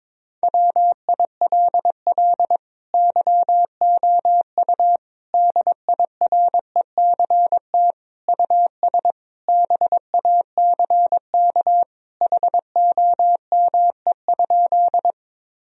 Morse code, heard as a keyed tone, about 22 wpm, 700 Hz. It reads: WILL YOU DIRECT US BACK HOME?
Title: Sentences from Top 1000 Words - Review Part 2 - 22wpm